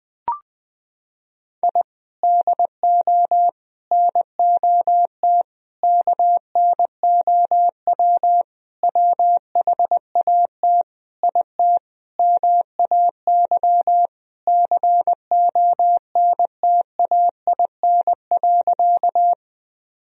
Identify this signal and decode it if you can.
Morse code, heard as a keyed tone, about 20 wpm, 700 Hz. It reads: I DO NOT KNOW WHAT IT MAY CONTAIN.
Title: Sentences from Top 1000 Words - Review Part 2 - 20wpm